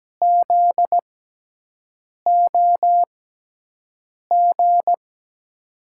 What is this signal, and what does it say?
Morse code, heard as a keyed tone, about 17 wpm, spaced as for 6 wpm, 700 Hz. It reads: ZOG
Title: USA Call Signs - 6 Characters - 6x Character Spacing - 17wpm